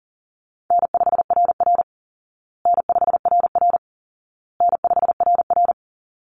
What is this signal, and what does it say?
Morse code, heard as a keyed tone, about 40 wpm, 700 Hz. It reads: N5RR N5RR N5RR
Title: Call Signs Repeated 2x - 40wpm